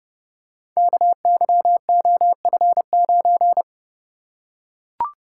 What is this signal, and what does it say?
Morse code, heard as a keyed tone, about 30 wpm, 700 Hz. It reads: KYOF9
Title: Code Groups - 5 Letter:Numbers - 30wpm